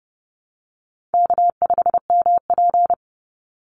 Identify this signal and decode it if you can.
Morse code, heard as a keyed tone, about 30 wpm, 700 Hz. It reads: K5MP
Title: Easy Call Signs Repeated 3x - 30wpm